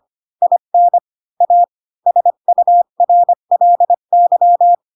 Morse code, heard as a keyed tone, about 25 wpm, 700 Hz. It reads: IN A SURLY